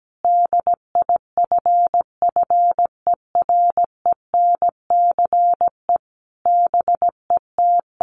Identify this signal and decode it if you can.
Morse code, heard as a keyed tone, about 17 wpm, 700 Hz. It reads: DIFFERENCE BET
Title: Sentences from Top 800 Words 17wpm